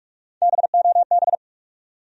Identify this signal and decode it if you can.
Morse code, heard as a keyed tone, about 46 wpm, 700 Hz. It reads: BOB
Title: USA Common Names - Max 3 Letters - 45wpm